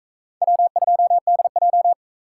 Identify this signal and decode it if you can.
Morse code, heard as a keyed tone, about 44 wpm, 700 Hz. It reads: W2DJ